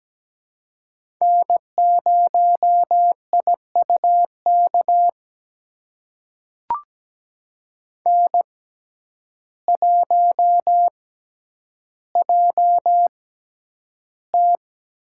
Morse code, heard as a keyed tone, about 17 wpm, spaced as for 9 wpm, 700 Hz. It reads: N0IUK N1JT
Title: USA Call Signs - 5 Characters - 6x Character Spacing - 17wpm